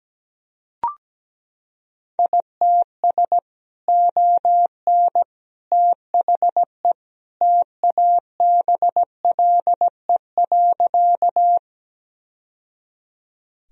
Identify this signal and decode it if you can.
Morse code, heard as a keyed tone, about 17 wpm, 700 Hz. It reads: ITS ON THE TABLE.
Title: Sentences from Top 1000 Words - Review Part 2 - 17wpm